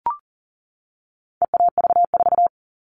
Morse code, heard as a keyed tone, about 40 wpm, 700 Hz. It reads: EAV4
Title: All Cut Numbers - 4-Digit 40wpm